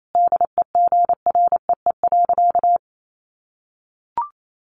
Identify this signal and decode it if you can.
Morse code, heard as a keyed tone, about 28 wpm, 700 Hz. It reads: DEGREE.
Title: Sentences from Top 1500 Words 28wpm